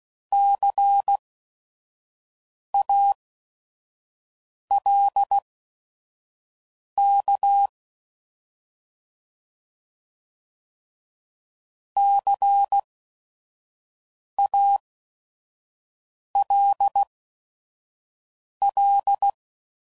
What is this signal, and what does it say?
Morse code, heard as a keyed tone, about 16 wpm, spaced as for 5 wpm, 800 Hz. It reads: CALK CALL
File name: Prefixes / By First Letter / Cs